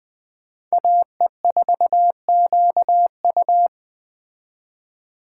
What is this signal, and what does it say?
Morse code, heard as a keyed tone, about 20 wpm, 700 Hz. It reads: AE4QU
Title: Speed Racing - USA Call Signs 17wpm